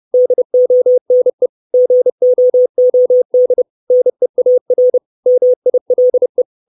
Morse code, heard as a keyed tone, about 30 wpm, 500 Hz. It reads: DONE GOOD NEAR MILE
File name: 632 Words 30WPM